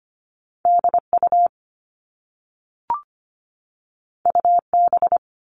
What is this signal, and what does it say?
Morse code, heard as a keyed tone, about 25 wpm, 700 Hz. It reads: DU UB
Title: All Cut Numbers - 2-Digit 25wpm